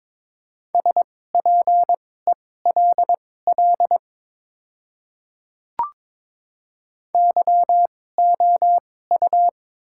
Morse code, heard as a keyed tone, about 22 wpm, spaced as for 16 wpm, 700 Hz. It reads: SPELL YOU
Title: 2x Character Spacing - Top 200 Words - 22wpm